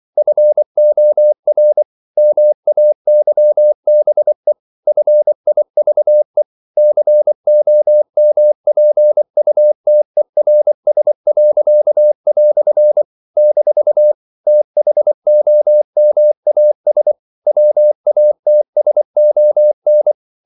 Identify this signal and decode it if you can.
Morse code, heard as a keyed tone, about 24 wpm, 600 Hz. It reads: FOR MAYBE FIVE COMPUTERS." - THOMAS WATSON
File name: CW / Practice / EnglishQuotations0424